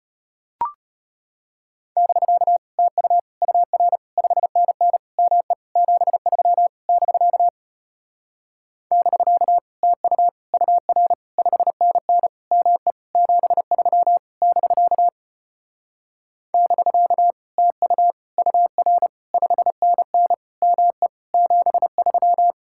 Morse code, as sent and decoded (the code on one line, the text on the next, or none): -...-.- / - ..- / ..- .-. / ..... -. -. / -- . / --... ...-- / -...-.- / -...-.- / - ..- / ..- .-. / ..... -. -. / -- . / --... ...-- / -...-.- / -...-.- / - ..- / ..- .-. / ..... -. -. / -- . / --... ...--
<BK> TU UR 5NN ME 73 <BK> <BK> TU UR 5NN ME 73 <BK> <BK> TU UR 5NN ME 73